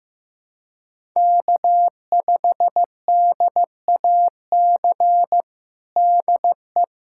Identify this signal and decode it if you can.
Morse code, heard as a keyed tone, about 15 wpm, 700 Hz. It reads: K5DAC DE
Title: Practice QSO - Exchange 4 - Ending - 15wpm